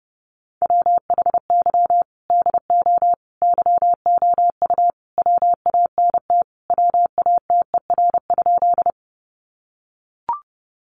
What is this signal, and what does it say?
Morse code, heard as a keyed tone, about 30 wpm, 700 Hz. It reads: WHY DO YOU WANT WATER?
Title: Sentences from Top 200 Words 30wpm